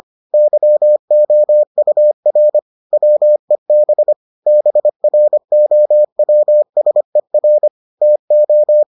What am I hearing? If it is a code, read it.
Morse code, heard as a keyed tone, about 25 wpm, 600 Hz. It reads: YOUR WEB BROWSER TO